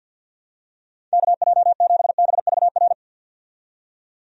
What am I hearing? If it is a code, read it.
Morse code, heard as a keyed tone, about 51 wpm, 700 Hz. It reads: KJ7BVR
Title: USA Call Signs - 6 Characters - 50wpm